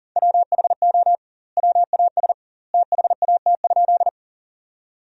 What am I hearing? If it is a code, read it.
Morse code, heard as a keyed tone, about 40 wpm, 700 Hz. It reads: WHO WAS THAT?
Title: 2x Word-Spacing - Sentences from Top 100 Words - 40wpm